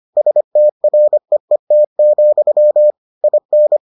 Morse code, heard as a keyed tone, about 25 wpm, 600 Hz. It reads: STREET, IN